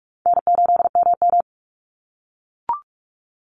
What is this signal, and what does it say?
Morse code, heard as a keyed tone, about 44 wpm, 700 Hz. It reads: N8MM
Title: USA Call Signs 45wpm